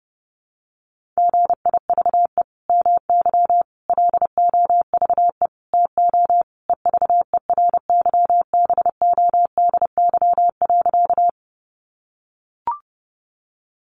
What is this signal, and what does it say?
Morse code, heard as a keyed tone, about 30 wpm, 700 Hz. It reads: GIVE MY LOVE TO EVERYBODY.
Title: Sentences from Top 1400 Words 30wpm